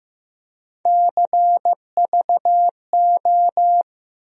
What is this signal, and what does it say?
Morse code, heard as a keyed tone, about 15 wpm, 700 Hz. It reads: CVO
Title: USA Call Sign Postfixes - 3 Letters 15wpm